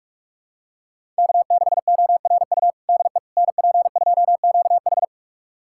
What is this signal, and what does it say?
Morse code, heard as a keyed tone, about 46 wpm, 700 Hz. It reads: K6ORU DE NP2QH
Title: USA Callsign DE Callsign - 5 Characters - 45wpm